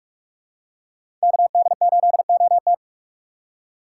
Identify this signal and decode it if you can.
Morse code, heard as a keyed tone, about 42 wpm, 700 Hz. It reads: KD8OT
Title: USA Call Signs - 5 Characters - 3x Character Spacing - 45wpm